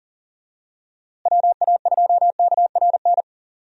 Morse code, heard as a keyed tone, about 40 wpm, 700 Hz. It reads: WA2KRN
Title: USA Call Signs - 6 Characters - Repeated 2x - 40wpm